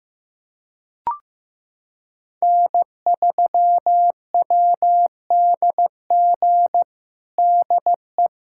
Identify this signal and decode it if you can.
Morse code, heard as a keyed tone, about 16 wpm, 700 Hz. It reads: N3WDG DE
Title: USA Callsign DE Callsign - 5 Characters - 15wpm